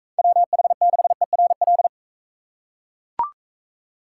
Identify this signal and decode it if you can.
Morse code, heard as a keyed tone, about 40 wpm, 700 Hz. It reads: WH6ERL